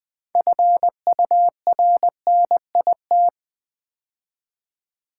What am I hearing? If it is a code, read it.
Morse code, heard as a keyed tone, about 20 wpm, 700 Hz. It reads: FURNIT
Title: Build A Word - 3 Letters At A Time - 9-Letter Words - 20wpm